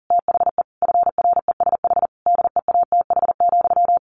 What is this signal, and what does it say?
Morse code, heard as a keyed tone, about 40 wpm, 700 Hz. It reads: THE FRESH DEATH,
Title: A Sound Of Thunder 40wpm